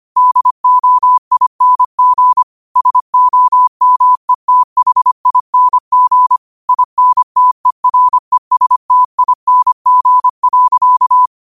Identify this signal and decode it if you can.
Morse code, heard as a keyed tone, about 25 wpm, 1000 Hz. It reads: DOING SOMETHING INTERESTING.
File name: qotdfiles / 25wpm